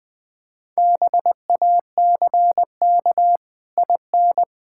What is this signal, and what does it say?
Morse code, heard as a keyed tone, about 20 wpm, 700 Hz. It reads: BACK IN
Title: Pangram Challenge v2 - 20wpm